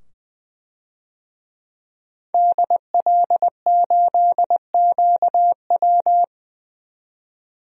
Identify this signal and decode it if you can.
Morse code, heard as a keyed tone, about 20 wpm, 700 Hz. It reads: DL8QW
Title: DXCC-100 Call Signs - 5 Characters - 20wpm